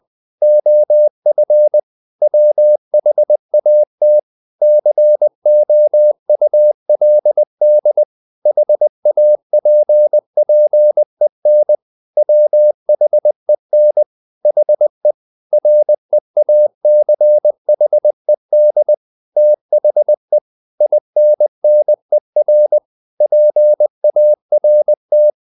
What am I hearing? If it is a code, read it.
Morse code, heard as a keyed tone, about 20 wpm, 600 Hz. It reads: OF WHAT COULD HAPPEN WHEN HE REACHED THE INNER PART
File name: the old man and the sea part 2 20wpm